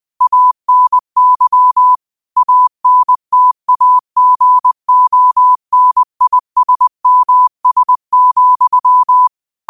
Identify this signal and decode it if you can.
Morse code, heard as a keyed tone, about 20 wpm, 1000 Hz. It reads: ANY ANTAGONISMS,
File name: qotdfiles / 20wpm